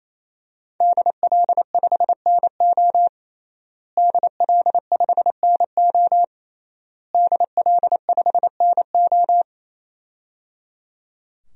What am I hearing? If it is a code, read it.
Morse code, heard as a keyed tone, about 28 wpm, 700 Hz. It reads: DL5NO DL5NO DL5NO
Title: DXCC-100 Call Signs - 5 Characters - Repeated 3x - 28wpm